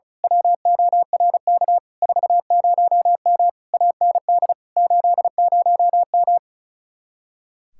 Morse code, heard as a keyed tone, about 33 wpm, 700 Hz. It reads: WORK 40M AND 80M
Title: Speed-Racing Common QSO Phrases - 35wpm